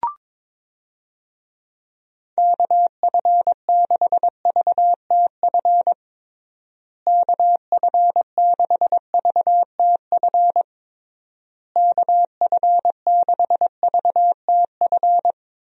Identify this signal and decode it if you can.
Morse code, heard as a keyed tone, about 22 wpm, 700 Hz. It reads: KF6VTF KF6VTF KF6VTF
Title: USA Call Signs - 6 Characters - Repeated 3x - 22wpm